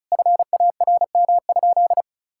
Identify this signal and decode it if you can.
Morse code, heard as a keyed tone, about 36 wpm, 700 Hz. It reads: FARM?
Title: Sentences from Top 1000 Words - Review Part 6 - 35wpm